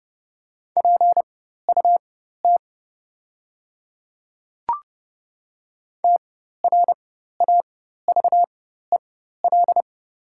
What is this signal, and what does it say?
Morse code, heard as a keyed tone, about 30 wpm, spaced as for 14 wpm, 700 Hz. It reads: PUT TRAVEL
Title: 4x Character Spacing - Top 500 Words - 30wpm